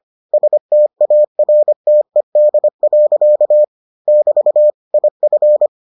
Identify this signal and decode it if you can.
Morse code, heard as a keyed tone, about 25 wpm, 600 Hz. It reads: STARTED. <BT> IF